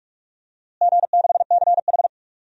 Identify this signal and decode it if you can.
Morse code, heard as a keyed tone, about 42 wpm, 700 Hz. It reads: G6CH
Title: DXCC-100 Call Signs - 4 Characters - Repeated 2x - 45wpm